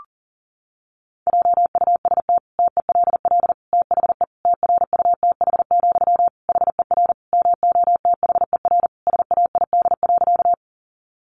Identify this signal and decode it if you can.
Morse code, heard as a keyed tone, about 40 wpm, 700 Hz. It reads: JUST TELL THE TRUTH, HER MOTHER SAID.
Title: Sentences from Top 800 Words 40wpm